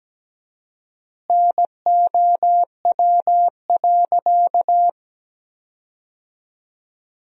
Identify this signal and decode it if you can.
Morse code, heard as a keyed tone, about 17 wpm, 700 Hz. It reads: NOW.
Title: 4x Word-Spacing - Sentences from Top 300 Words - 17wpm